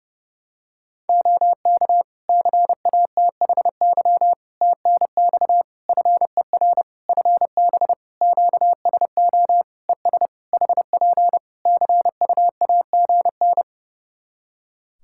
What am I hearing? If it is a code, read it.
Morse code, heard as a keyed tone, about 32 wpm, 700 Hz. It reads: OK CATHY TNX FER FB QSO ES HP CUAGN
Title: Practice QSO - Exchange 4 - Ending - 30wpm